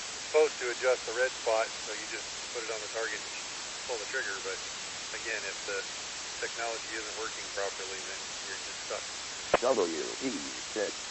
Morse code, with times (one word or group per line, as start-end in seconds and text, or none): none